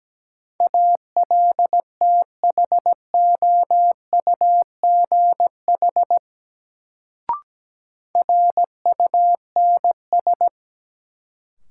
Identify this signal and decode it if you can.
Morse code, heard as a keyed tone, about 17 wpm, 700 Hz. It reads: ALTHOUGH RUNS
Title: Sentences from Top 600 Words - New Word Drill - 17wpm